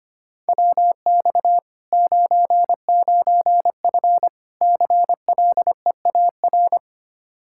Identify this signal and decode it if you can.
Morse code, heard as a keyed tone, about 25 wpm, 700 Hz. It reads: WX 99F CLEAR
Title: Practice QSO - Exchange 2 - Info - 25wpm